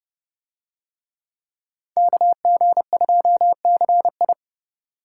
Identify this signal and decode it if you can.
Morse code, heard as a keyed tone, about 29 wpm, 700 Hz. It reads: KG2CI